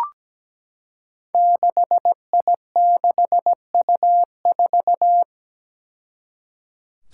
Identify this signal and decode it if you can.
Morse code, heard as a keyed tone, about 17 wpm, 700 Hz. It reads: 6I6U4
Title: Five Challenging Letter-Number Combinations 17wpm